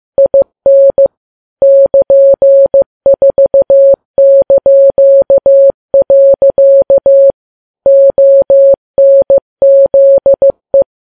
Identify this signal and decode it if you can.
Morse code, heard as a keyed tone, about 15 wpm, 550 Hz. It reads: IN <KN>4). ONZE
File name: Files Web tls / Morse on9fv / morse15kps